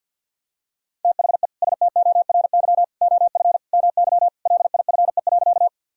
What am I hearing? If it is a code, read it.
Morse code, heard as a keyed tone, about 52 wpm, 700 Hz. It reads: THE STORY OF MY LIFE.
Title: Sentences from Top 1000 Words - Review Part 1 - 50wpm